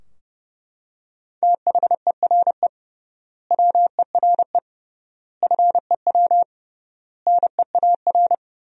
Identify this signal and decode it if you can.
Morse code, heard as a keyed tone, about 32 wpm, 700 Hz. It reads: THERE WERE FEW NEAR
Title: 3x Word-Spacing - Sentences from Top 300 Words - 30wpm